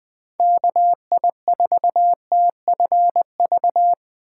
Morse code, heard as a keyed tone, about 20 wpm, 700 Hz. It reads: KI4TFV
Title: USA Call Signs - 6 Characters - 8x Character Spacing - 20wpm